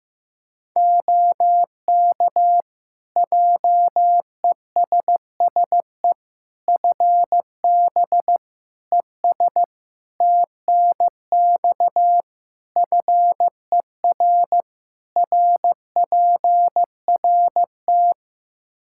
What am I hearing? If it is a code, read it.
Morse code, heard as a keyed tone, about 15 wpm, 700 Hz. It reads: OK JESSE FB ES TNX FER RPRT
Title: Practice QSO - Exchange 2 - Info - 15wpm